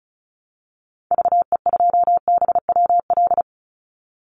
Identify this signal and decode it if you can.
Morse code, heard as a keyed tone, about 34 wpm, 700 Hz. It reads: VE2BWL
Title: Call Signs Repeated 2x - 35wpm